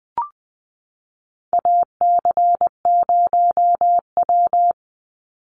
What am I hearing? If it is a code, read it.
Morse code, heard as a keyed tone, about 20 wpm, 700 Hz. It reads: AC0W
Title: Call Signs Repeated 2x - 20wpm